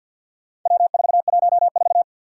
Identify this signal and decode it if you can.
Morse code, heard as a keyed tone, about 52 wpm, 700 Hz. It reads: W414
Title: Four Challenging Letter-Number Combinations 50wpm